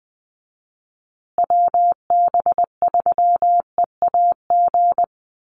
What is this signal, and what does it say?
Morse code, heard as a keyed tone, about 20 wpm, 700 Hz. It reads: WB3EAG
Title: USA Call Signs Repeated 3x 20wpm